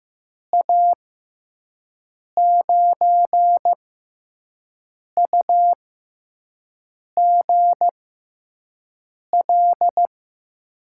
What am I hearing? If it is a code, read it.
Morse code, heard as a keyed tone, about 15 wpm, spaced as for 5 wpm, 700 Hz. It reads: A9UGL
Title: USA Call Signs - 6 Characters - 6x Character Spacing - 15wpm